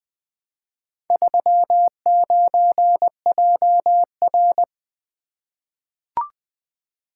Morse code, heard as a keyed tone, about 20 wpm, 700 Hz. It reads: 39JR
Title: Four Letter-Number Combinations - 20wpm